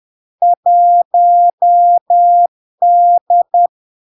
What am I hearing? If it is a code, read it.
Morse code, heard as a keyed tone, about 10 wpm, 700 Hz. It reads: 1D
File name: LARK-FD-2022-Log-in-CW-10wpm-700-Hz